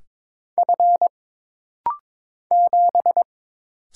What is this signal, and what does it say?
Morse code, heard as a keyed tone, about 22 wpm, 700 Hz. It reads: F 7
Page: This rapid-fire timing is geared to push your ICR (Instant Character Recognition) skills.